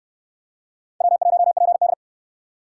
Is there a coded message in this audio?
Yes: Morse code from 1.0 to 1.9 s, heard as a keyed tone, about 69 wpm, 700 Hz. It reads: W1PD